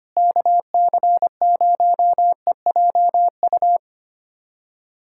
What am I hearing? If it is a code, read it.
Morse code, heard as a keyed tone, about 25 wpm, 700 Hz. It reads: KC0EJU
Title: USA Callsign DE Callsign - 6 Characters - 25wpm